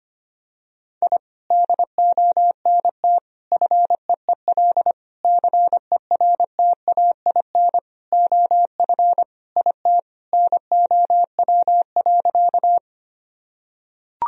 Morse code, heard as a keyed tone, about 25 wpm, 700 Hz. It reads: I DONT FEEL CERTAIN OF IT NOW.
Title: Sentences from Top 1000 Words - Review Part 2 - 25wpm